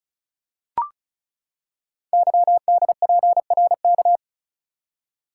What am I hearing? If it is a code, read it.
Morse code, heard as a keyed tone, about 36 wpm, 690 Hz. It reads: YDPRK
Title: Code Groups - 5 Letters - 35wpm